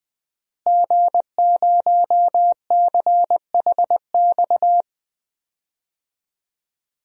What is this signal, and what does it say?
Morse code, heard as a keyed tone, about 20 wpm, 700 Hz. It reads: G0CHX
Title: DXCC-100 Call Signs - 5 Characters - Repeated 2x - 20wpm